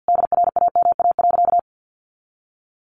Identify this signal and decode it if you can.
Morse code, heard as a keyed tone, about 48 wpm, 700 Hz. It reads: DRAMA.
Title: Sentences from Top 1500 Words 50wpm